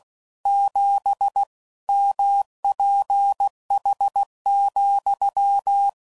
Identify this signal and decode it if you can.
Morse code, heard as a keyed tone, about 16 wpm, 800 Hz. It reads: 7 MPH,
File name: CW Code Groups @15WPM / Weather Sets / Weather 4 24